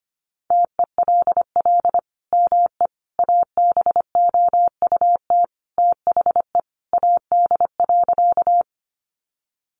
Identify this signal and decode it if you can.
Morse code, heard as a keyed tone, about 25 wpm, 700 Hz. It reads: TELL ME ABOUT THE AD.
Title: Sentences from Top 1300 Words 25wpm